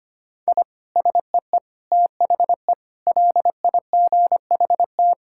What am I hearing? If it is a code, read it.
Morse code, heard as a keyed tone, about 25 wpm, 700 Hz. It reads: I SEE THE LIGHT
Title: Sentences from Top 1000 Words - Review Part 6 - 25wpm